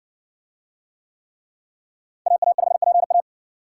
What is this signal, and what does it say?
Morse code, heard as a keyed tone, about 53 wpm, 700 Hz. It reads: AA5PA